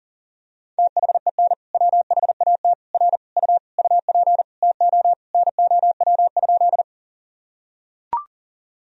Morse code, heard as a keyed tone, about 40 wpm, 700 Hz. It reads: THEN WHAT R U UP TO NOW?
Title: Sentences from Top 100 Words and Abbreviations 40wpm